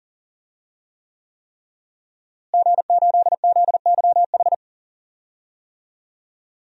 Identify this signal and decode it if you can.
Morse code, heard as a keyed tone, about 40 wpm, 700 Hz. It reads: G8ZYH
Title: DXCC-100 Call Signs - 5 Characters - Repeated 3x - 40wpm